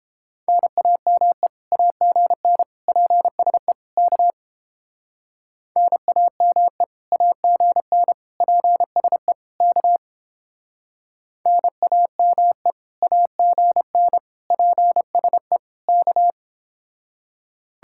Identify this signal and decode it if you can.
Morse code, heard as a keyed tone, about 31 wpm, 700 Hz. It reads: NAME AGN PSE K NAME AGN PSE K NAME AGN PSE K